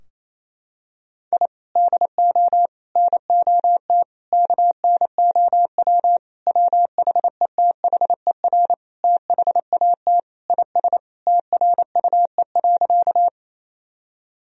Morse code, heard as a keyed tone, about 28 wpm, 700 Hz. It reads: I DO NOT KNOW WHETHER THAT IS TRUE.
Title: Sentences from Top 1000 Words - Review Part 3 - 28wpm